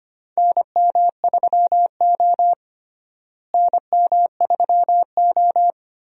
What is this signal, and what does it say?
Morse code, heard as a keyed tone, about 25 wpm, 700 Hz. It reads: NM3O NM3O
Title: USA Call Signs - 4 Characters - Repeated 3x - 25wpm